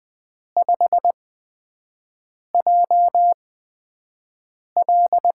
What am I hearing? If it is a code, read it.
Morse code, heard as a keyed tone, about 20 wpm, spaced as for 5 wpm, 700 Hz. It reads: 5JL
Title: USA Call Signs - 8x Character Spacing - 20wpm